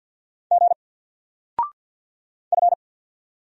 Morse code, heard as a keyed tone, about 50 wpm, 700 Hz. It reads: G F
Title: Single Letter-Number 50wpm